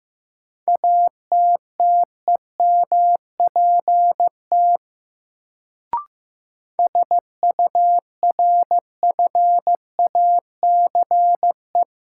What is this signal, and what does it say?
Morse code, heard as a keyed tone, about 16 wpm, 700 Hz. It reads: ATTEMPT SURFACE
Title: Sets of 4 Words - New Word Drill - 15wpm